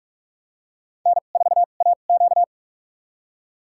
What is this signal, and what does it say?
Morse code, heard as a keyed tone, about 44 wpm, spaced as for 32 wpm, 690 Hz. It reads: N4AQ
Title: USA Call Signs - 4 Characters - 2x Character Spacing - 45wpm